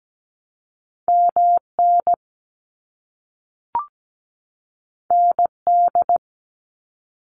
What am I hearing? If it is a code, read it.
Morse code, heard as a keyed tone, about 17 wpm, 700 Hz. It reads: MN ND
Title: US States 17wpm